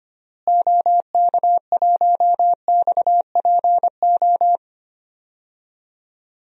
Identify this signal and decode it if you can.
Morse code, heard as a keyed tone, about 25 wpm, 700 Hz. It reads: OK1XPO
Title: DXCC-100 Call Signs - 6 Characters - Repeated 2x - 25wpm